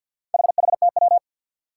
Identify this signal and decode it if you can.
Morse code, heard as a keyed tone, about 52 wpm, 700 Hz. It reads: HHTW
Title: Four Challenging Letter-Number Combinations 50wpm